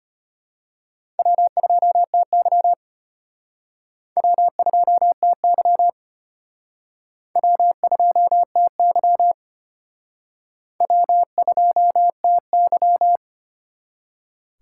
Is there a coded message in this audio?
Yes: Morse code from 1.2 to 13.2 s, heard as a keyed tone, about 32 wpm, 700 Hz. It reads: W2TY W2TY W2TY W2TY